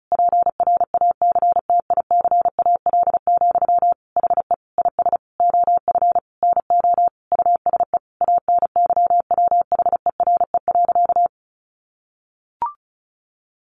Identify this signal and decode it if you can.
Morse code, heard as a keyed tone, about 34 wpm, 700 Hz. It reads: PRACTICAL, HE IS OF NO USE ANYWHERE.
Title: Sentences from Top 1400 Words 35wpm